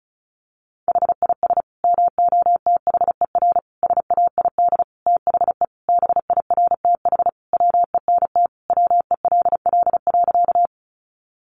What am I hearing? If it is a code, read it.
Morse code, heard as a keyed tone, about 34 wpm, 700 Hz. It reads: HIS MOTHER SAID THE BIRTH WENT WELL.
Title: Sentences from Top 800 Words 35wpm